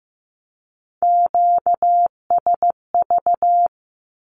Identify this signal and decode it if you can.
Morse code, heard as a keyed tone, about 15 wpm, 700 Hz. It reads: QSV
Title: Q-Codes Repeated 3x 15wpm